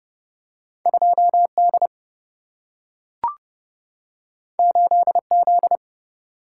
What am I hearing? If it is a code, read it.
Morse code, heard as a keyed tone, about 30 wpm, 700 Hz. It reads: 2 D 8 Z
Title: Single-Letter Single-Number 30wpm